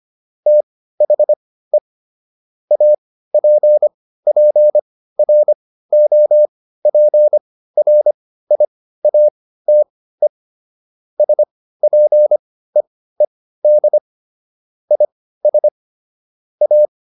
Morse code, heard as a keyed tone, about 25 wpm, spaced as for 15 wpm, 600 Hz. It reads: THE APPROPRIATE SPEED IS A